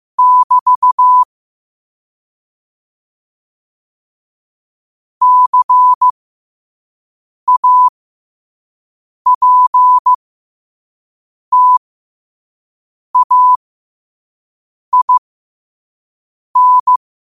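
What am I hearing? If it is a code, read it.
Morse code, heard as a keyed tone, about 15 wpm, spaced as for 5 wpm, 1000 Hz. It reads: <BT> CAPTAIN
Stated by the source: Morse sent at 5 wpm.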